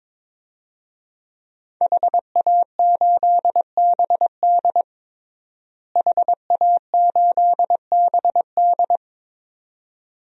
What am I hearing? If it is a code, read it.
Morse code, heard as a keyed tone, about 22 wpm, 700 Hz. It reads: HA8BD HA8BD
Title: DXCC-100 Call Signs - 5 Characters - Repeated 2x - 22wpm